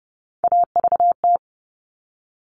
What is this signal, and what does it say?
Morse code, heard as a keyed tone, about 30 wpm, 700 Hz. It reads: AVT
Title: All Cut Numbers - 3-Digit 30wpm